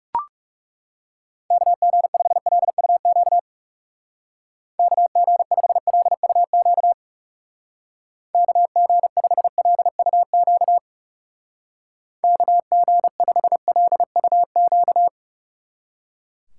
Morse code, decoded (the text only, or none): KG5LUQ KG5LUQ KG5LUQ KG5LUQ